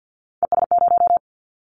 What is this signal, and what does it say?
Morse code, heard as a keyed tone, about 50 wpm, 700 Hz. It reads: ES0
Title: Callsign Prefixes for Top 50 DXCC - 50wpm